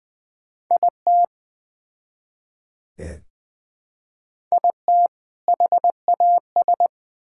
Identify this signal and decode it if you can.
Morse code, heard as a keyed tone, about 20 wpm, 700 Hz. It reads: IT IT HAS
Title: Build A Phrase - 3-Word Phrases - 20wpm